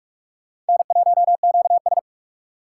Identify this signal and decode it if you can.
Morse code, heard as a keyed tone, about 58 wpm, 700 Hz. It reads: N1QS